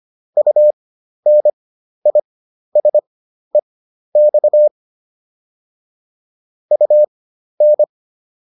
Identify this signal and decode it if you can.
Morse code, heard as a keyed tone, about 25 wpm, spaced as for 12 wpm, 600 Hz. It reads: UNISEX UN